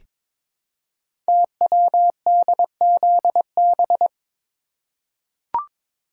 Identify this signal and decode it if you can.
Morse code, heard as a keyed tone, about 22 wpm, 700 Hz. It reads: TWDZB
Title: Code Groups - 5 Letters - 22wpm